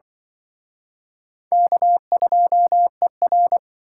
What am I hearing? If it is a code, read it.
Morse code, heard as a keyed tone, about 25 wpm, 700 Hz. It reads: K2ER
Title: Speed Racing - USA Call Signs 20wpm